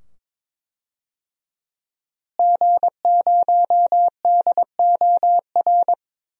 Morse code, heard as a keyed tone, about 22 wpm, 700 Hz. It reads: G0DOR
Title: DXCC-100 Call Signs - 5 Characters - Repeated 2x - 22wpm